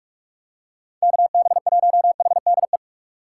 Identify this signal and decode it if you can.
Morse code, heard as a keyed tone, about 42 wpm, 700 Hz. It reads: KB1HDE
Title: USA Call Signs - 6 Characters - 6x Character Spacing - 45wpm